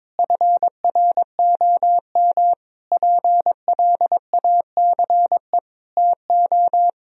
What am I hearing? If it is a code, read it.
Morse code, heard as a keyed tone, about 21 wpm, 700 Hz. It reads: FROM PLACE TO
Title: Sentences from Top 1000 Words - Review Part 1 - 22wpm